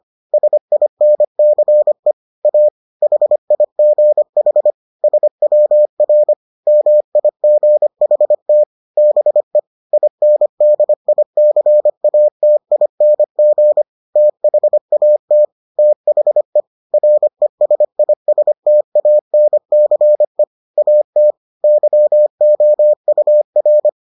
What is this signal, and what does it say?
Morse code, heard as a keyed tone, about 25 wpm, 600 Hz. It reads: SINCE A HIGH SWR MIGHT BE INDICATING THAT THE RESISTANCE AT YOUR